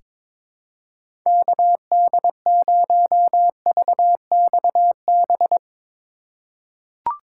Morse code, heard as a keyed tone, about 23 wpm, 700 Hz. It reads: KD0VXB
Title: USA Call Signs - 6 Characters - 4x Character Spacing - 22wpm